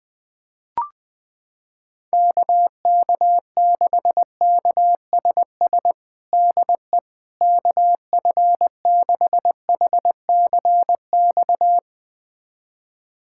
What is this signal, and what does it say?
Morse code, heard as a keyed tone, about 20 wpm, 700 Hz. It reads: KK6KSS DE KF6HCX
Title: USA Callsign DE Callsign - 6 Characters - 20wpm